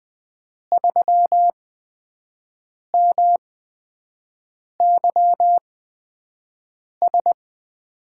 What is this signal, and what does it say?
Morse code, heard as a keyed tone, about 20 wpm, spaced as for 5 wpm, 700 Hz. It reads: 3MYS
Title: USA Call Signs - 5 Characters - 8x Character Spacing - 20wpm